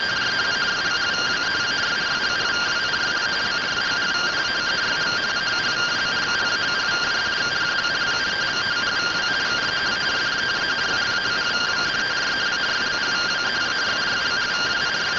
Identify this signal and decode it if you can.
Two-tone teleprinter signal, about 170 Hz shift.